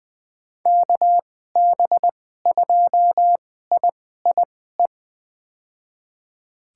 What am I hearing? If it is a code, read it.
Morse code, heard as a keyed tone, about 20 wpm, spaced as for 14 wpm, 700 Hz. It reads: KB2IIE
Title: USA Call Signs - 6 Characters - 2x Character Spacing - 20wpm